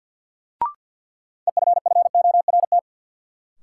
Morse code, heard as a keyed tone, about 48 wpm, 700 Hz. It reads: EFFORT
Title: Sentences from Top 1200 Words - New Word Drill Alt - 50wpm